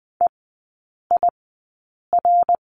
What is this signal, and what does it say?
Morse code, heard as a keyed tone, about 20 wpm, spaced as for 8 wpm, 700 Hz. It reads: EIR
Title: 053 Lesson 10e - Letter r - Three Characters 20wpm